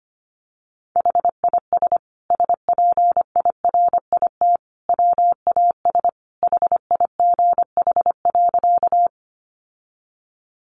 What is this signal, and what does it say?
Morse code, heard as a keyed tone, about 25 wpm, 700 Hz. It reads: HIS SPIRIT WAS HIGH.
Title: Sentences from Top 1400 Words 25wpm